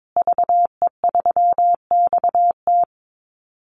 Morse code, heard as a keyed tone, about 22 wpm, 700 Hz. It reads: VE3XT
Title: Call Signs Repeated 2x - 22wpm